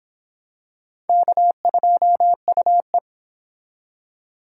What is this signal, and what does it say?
Morse code, heard as a keyed tone, about 26 wpm, 700 Hz. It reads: K2UE